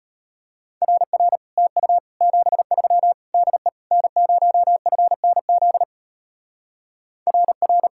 Morse code, heard as a keyed tone, about 33 wpm, 700 Hz. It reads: RR TU 73 DE N0FNZ RR